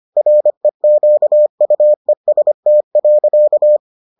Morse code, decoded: REQUEST.

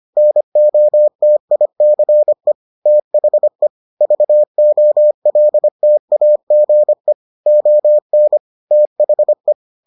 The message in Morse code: NOTICE THE VOLTAGE ON THE

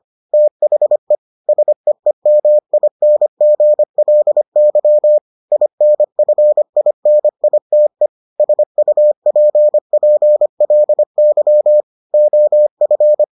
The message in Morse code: THE SEEMINGLY INFINITE SUPPLY OF

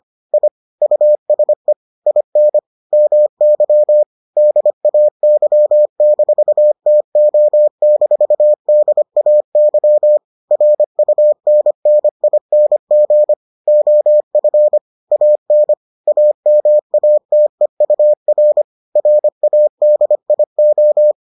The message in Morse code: I USE IN MY DAY-TO-DAY RUNNING OF AN AMATEUR RADIO

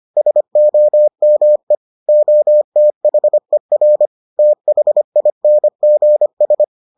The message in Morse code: SOME OTHER THINGS

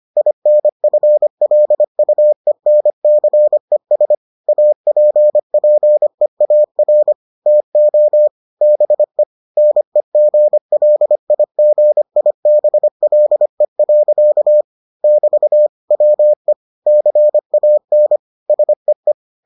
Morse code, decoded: INFLUENCES APPEAR TO BE NEGLIGIBLE. <BT> WE CAN SEE

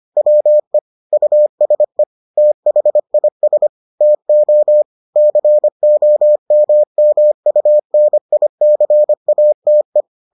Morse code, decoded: WE USE THIS TO COMMUNICATE

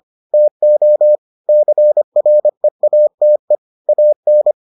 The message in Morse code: TO CREATE AN

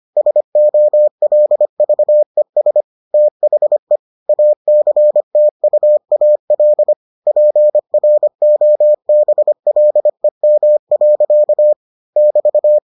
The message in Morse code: SOLVES THE ACTUAL PROBLEM. <BT>

